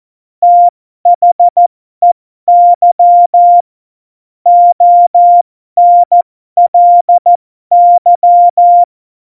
Morse code, heard as a keyed tone, about 14 wpm, spaced as for 12 wpm, 700 Hz. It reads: THEY ONLY
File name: Morsekurs / Witze